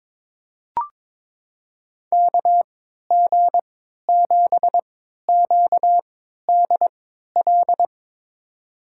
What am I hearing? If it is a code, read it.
Morse code, heard as a keyed tone, about 22 wpm, spaced as for 12 wpm, 700 Hz. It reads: KG7QDL
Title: USA Call Signs - 6 Characters - 3x Character Spacing - 22wpm